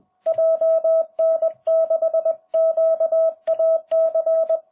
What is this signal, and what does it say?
Morse code, heard as a keyed tone, about 20 wpm, 650 Hz. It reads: JN6QAC